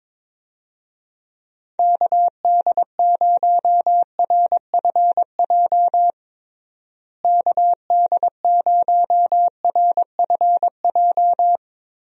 Morse code, heard as a keyed tone, about 22 wpm, 700 Hz. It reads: KD0RFJ KD0RFJ
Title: USA Call Signs - 6 Characters - Repeated 3x - 22wpm